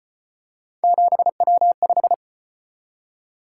Morse code, heard as a keyed tone, about 32 wpm, 700 Hz. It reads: 7W5